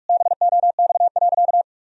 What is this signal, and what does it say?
Morse code, heard as a keyed tone, about 46 wpm, 700 Hz. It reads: BOX.
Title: Sentences from Top 1000 Words - Review Part 2 - 45wpm